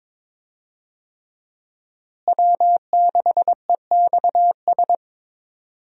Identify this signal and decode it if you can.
Morse code, heard as a keyed tone, about 22 wpm, 700 Hz. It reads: W6EXS
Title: USA Call Signs - 5 Characters - Repeated 3x - 22wpm